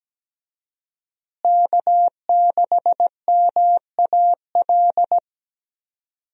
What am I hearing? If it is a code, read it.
Morse code, heard as a keyed tone, about 17 wpm, 700 Hz. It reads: K6MAL
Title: USA Call Signs - 5 Characters - Repeated 3x - 17wpm